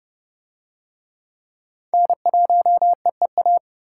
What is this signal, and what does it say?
Morse code, heard as a keyed tone, about 28 wpm, 700 Hz. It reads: N1EEA